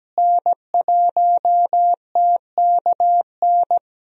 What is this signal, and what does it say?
Morse code, heard as a keyed tone, about 17 wpm, 700 Hz. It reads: N1TKN
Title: USA Call Signs - 5 Characters - Repeated 2x - 17wpm